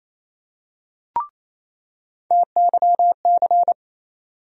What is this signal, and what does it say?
Morse code, heard as a keyed tone, about 28 wpm, 700 Hz. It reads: TYC
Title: USA Call Sign Postfixes - 3 Letters 28wpm